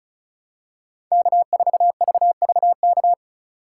Morse code, heard as a keyed tone, about 36 wpm, 700 Hz. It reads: K4VVK
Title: USA Call Signs - 5 Characters - 3x Character Spacing - 35wpm